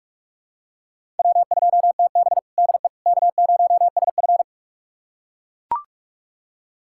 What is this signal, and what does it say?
Morse code, heard as a keyed tone, about 41 wpm, 700 Hz. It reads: W2TB DE K0SF
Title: USA Callsign DE Callsign - 4 Characters - 45wpm